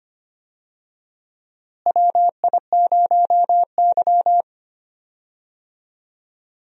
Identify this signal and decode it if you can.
Morse code, heard as a keyed tone, about 25 wpm, 700 Hz. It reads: WI0Y
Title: USA Call Signs - 4 Characters - Repeated 2x - 25wpm